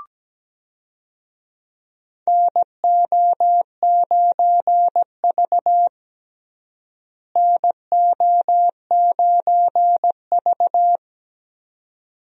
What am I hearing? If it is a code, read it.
Morse code, heard as a keyed tone, about 17 wpm, 700 Hz. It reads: NO9V NO9V
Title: USA Call Signs - 4 Characters - Repeated 2x - 17wpm